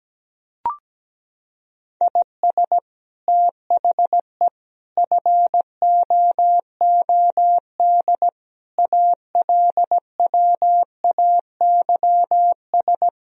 Morse code, heard as a keyed tone, about 17 wpm, 700 Hz. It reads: IS THE FOOD ALWAYS
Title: Sentences from Top 1000 Words - Review Part 1 - 17wpm